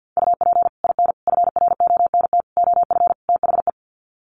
Morse code, heard as a keyed tone, about 50 wpm, 700 Hz. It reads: UP IN FRONT OF THE
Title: Sets of 5 Words Encore2 50wpm